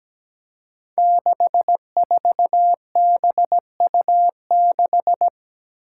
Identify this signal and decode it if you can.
Morse code, heard as a keyed tone, about 17 wpm, 700 Hz. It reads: 64BU6
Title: Five Challenging Letter-Number Combinations 17wpm